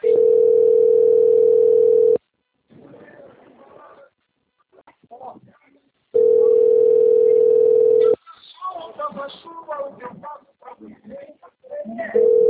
Ringback tone (North America), 440 + 480 Hz.